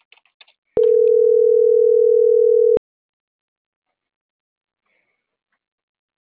Ringback tone (North America), 440 + 480 Hz.